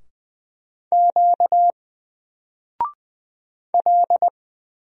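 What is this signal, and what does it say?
Morse code, heard as a keyed tone, about 20 wpm, 700 Hz. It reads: Q L
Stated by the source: Single letter and number Morse drill at 20 wpm, from N4LCD.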